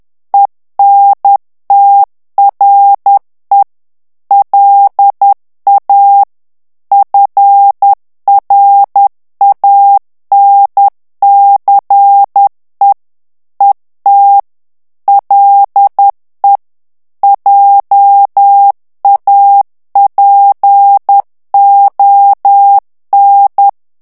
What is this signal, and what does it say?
Morse code, heard as a keyed tone, about 10 wpm, 800 Hz. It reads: ENTRE LA FRANCE ET LE JAPON
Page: Français - texte en clair 1 - vitesse : 10 mots minute